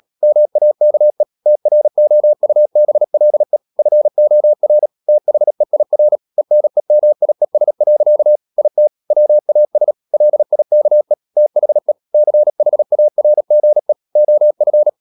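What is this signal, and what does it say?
Morse code, heard as a keyed tone, about 37 wpm, 600 Hz. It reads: MAKE TROUBLE FOR THEIR ENEMIES. IT WAS LIKE THE CHARGE OF
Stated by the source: Morse code practice at 35–39 wpm sending English text of Ken Follett's novel Never, from CW med Gnister.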